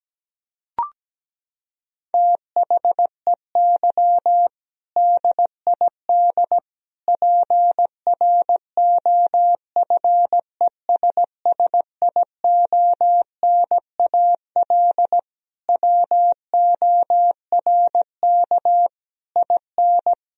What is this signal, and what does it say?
Morse code, heard as a keyed tone, about 17 wpm, 700 Hz. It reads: THEY DID PROFESSIONAL WORK IN
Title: Sentences from Top 1000 Words - Review Part 4 - 17wpm